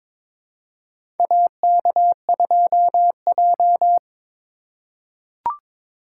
Morse code, heard as a keyed tone, about 21 wpm, 700 Hz. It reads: AK2J
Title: USA Call Signs - 4 Characters - 4x Character Spacing - 22wpm